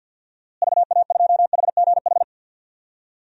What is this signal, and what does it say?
Morse code, heard as a keyed tone, about 53 wpm, 700 Hz. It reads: VA2HGH
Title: DXCC-100 Call Signs - 6 Characters - Repeated 2x - 50wpm